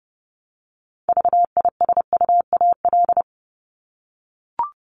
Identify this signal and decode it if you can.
Morse code, heard as a keyed tone, about 30 wpm, 700 Hz. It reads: VISUAL
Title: Top 751-1000 Words 30wpm